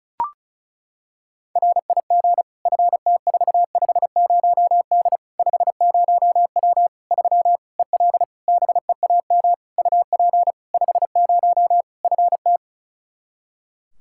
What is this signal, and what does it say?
Morse code, heard as a keyed tone, about 36 wpm, 700 Hz. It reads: RIG FT450D 50W 3 EL BEAM UP 50 FT
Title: K1USN SST QSO Practice 35wpm